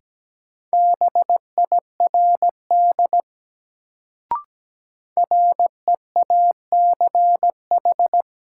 Morse code, heard as a keyed tone, about 17 wpm, 700 Hz. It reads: BIRD REACH
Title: Sentences from Top 400 Words - New Word Drill - 17wpm